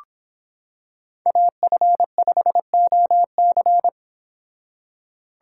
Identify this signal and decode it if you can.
Morse code, heard as a keyed tone, about 26 wpm, 700 Hz. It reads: AF5OC